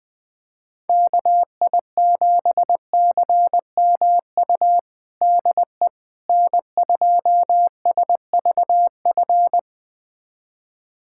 Morse code, heard as a keyed tone, about 20 wpm, 700 Hz. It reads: KI7CMU DE N2SVF
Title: USA Callsign DE Callsign 20wpm